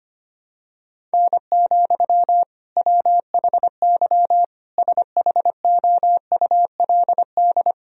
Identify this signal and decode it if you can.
Morse code, heard as a keyed tone, about 25 wpm, 700 Hz. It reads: N, WHY SHOULD
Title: Sentences from Top 300 Words and Abbreviations 25wpm